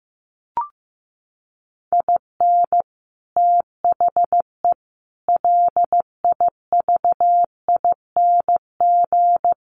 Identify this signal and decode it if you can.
Morse code, heard as a keyed tone, about 15 wpm, 700 Hz. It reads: IN THE LIVING
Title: Sets of 4 Words 15wpm